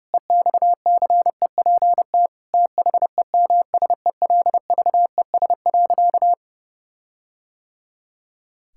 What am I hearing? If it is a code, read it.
Morse code, heard as a keyed tone, about 30 wpm, 700 Hz. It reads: EXCEPT THEMSELVES.
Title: Sentences from Top 1600 Words 30wpm